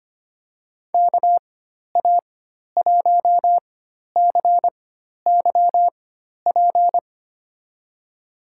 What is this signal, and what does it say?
Morse code, heard as a keyed tone, about 25 wpm, spaced as for 12 wpm, 700 Hz. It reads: KA1CYP
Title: USA Call Signs - 6 Characters - 4x Character Spacing - 25wpm